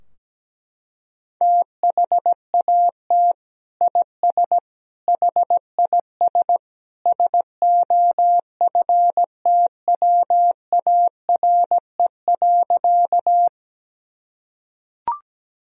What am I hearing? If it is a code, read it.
Morse code, heard as a keyed tone, about 17 wpm, 700 Hz. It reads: THAT IS HIS SOFTWARE.
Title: Sentences from Top 1000 Words - Review Part 3 - 17wpm